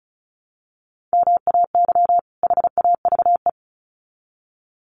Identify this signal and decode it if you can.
Morse code, heard as a keyed tone, about 34 wpm, 700 Hz. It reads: MAY HAVE
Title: Sets of 2 Words 35wpm